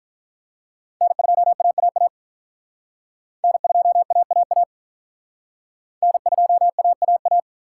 Morse code, heard as a keyed tone, about 44 wpm, 700 Hz. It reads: N2AAA N2AAA N2AAA